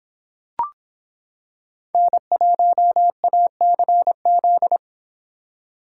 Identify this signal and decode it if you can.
Morse code, heard as a keyed tone, about 25 wpm, 700 Hz. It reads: N1ACZ